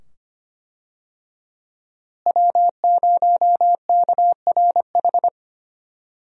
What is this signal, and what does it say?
Morse code, heard as a keyed tone, about 25 wpm, 700 Hz. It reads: W0KRH
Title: USA Call Signs - 5 Characters - Repeated 2x - 25wpm